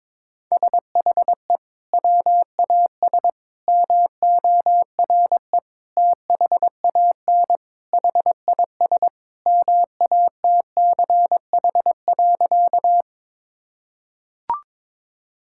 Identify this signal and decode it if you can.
Morse code, heard as a keyed tone, about 22 wpm, 700 Hz. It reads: SHE WAS MORE THAN HIS MATCH.
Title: Sentences from Top 1600 Words 22wpm